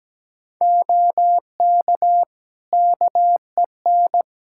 Morse code, heard as a keyed tone, about 17 wpm, 700 Hz. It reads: OK KEN
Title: Practice QSO - Exchange 3 - More Info - 17wpm